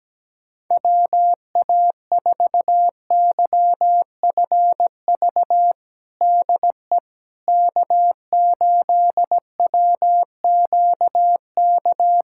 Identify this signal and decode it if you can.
Morse code, heard as a keyed tone, about 17 wpm, 700 Hz. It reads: WA4YFV DE K8WQK
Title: USA Callsign DE Callsign 17wpm